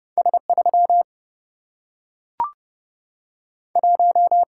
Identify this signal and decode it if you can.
Morse code, heard as a keyed tone, about 30 wpm, 700 Hz. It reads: S 3 1
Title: Single-Letter Single-Number 30wpm